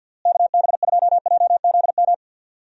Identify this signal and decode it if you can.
Morse code, heard as a keyed tone, about 47 wpm, 700 Hz. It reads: KB2JZM
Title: USA Call Signs - 6 Characters - Repeated 3x - 50wpm